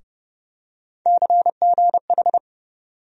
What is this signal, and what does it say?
Morse code, heard as a keyed tone, about 30 wpm, 700 Hz. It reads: CGH
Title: USA Call Sign Postfixes - 3 Letters 30wpm